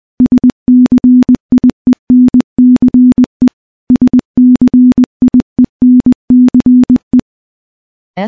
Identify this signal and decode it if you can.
Morse code, heard as a keyed tone, about 20 wpm, 250 Hz. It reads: SCIENCE SCIENCE
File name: morse / morse-training-20WPM